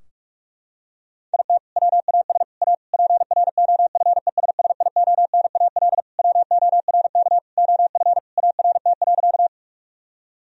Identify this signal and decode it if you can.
Morse code, heard as a keyed tone, about 46 wpm, 700 Hz. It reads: IT WAS A PROFESSIONAL WORK OF ART.
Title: Sentences from Top 1000 Words - Review Part 4 - 45wpm